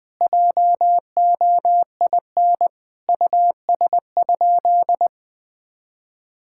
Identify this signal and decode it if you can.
Morse code, heard as a keyed tone, about 20 wpm, 700 Hz. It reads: JOIN US?
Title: Sentences with 4-Word Phrases - 20wpm